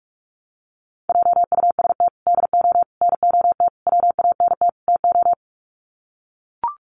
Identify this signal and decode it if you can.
Morse code, heard as a keyed tone, about 44 wpm, 700 Hz. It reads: JUST DO NOT WANT TO
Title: Sets of 5 Words Encore 45wpm